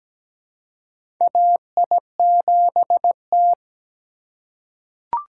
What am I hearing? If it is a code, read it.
Morse code, heard as a keyed tone, about 17 wpm, 700 Hz. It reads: AI7T
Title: USA Call Signs - 4x Character Spacing - 17wpm